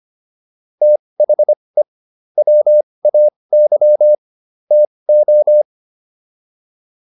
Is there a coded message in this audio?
Yes: Morse code from 0.8 to 5.6 s, heard as a keyed tone, about 25 wpm, spaced as for 20 wpm, 600 Hz. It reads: THE WAY TO